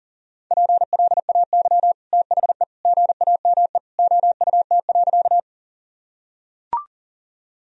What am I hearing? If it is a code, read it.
Morse code, heard as a keyed tone, about 40 wpm, 700 Hz. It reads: PLAY THE GAME OUT.
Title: Sentences from Top 1000 Words - Review Part 3 - 40wpm